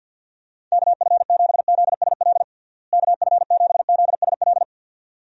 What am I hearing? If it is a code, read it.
Morse code, heard as a keyed tone, about 51 wpm, 700 Hz. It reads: KF7ZSL KF7ZSL
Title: USA Call Signs - 6 Characters - Repeated 2x - 50wpm